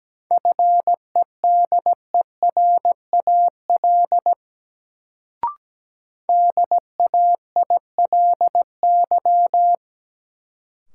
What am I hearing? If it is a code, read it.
Morse code, heard as a keyed tone, about 17 wpm, 700 Hz. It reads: FEDERAL DAILY
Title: Sentences from Top 1100 Words - New Word Drill Alt - 17wpm